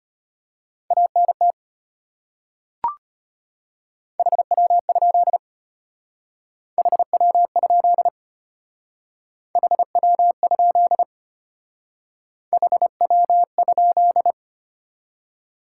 Morse code, heard as a keyed tone, about 32 wpm, 700 Hz. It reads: ANT HW? HW? HW? HW?